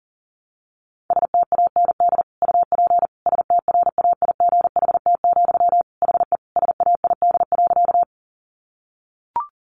Morse code, heard as a keyed tone, about 40 wpm, 700 Hz. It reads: STAND UP STRAIGHT, HE SAID.
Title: Sentences from Top 1000 Words 40wpm